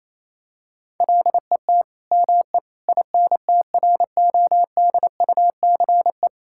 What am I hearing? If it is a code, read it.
Morse code, heard as a keyed tone, about 28 wpm, 700 Hz. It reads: LET ME INTRODUCE
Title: Sentences from Top 1600 Words 28wpm